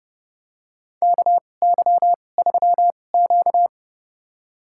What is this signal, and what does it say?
Morse code, heard as a keyed tone, about 30 wpm, spaced as for 22 wpm, 700 Hz. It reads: KY3Q
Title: USA Call Signs - 2x Character Spacing - 30wpm